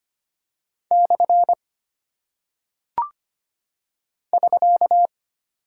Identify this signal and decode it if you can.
Morse code, heard as a keyed tone, about 25 wpm, 700 Hz. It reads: / <SK>
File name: Single Punctuation-Prosign - 25wpm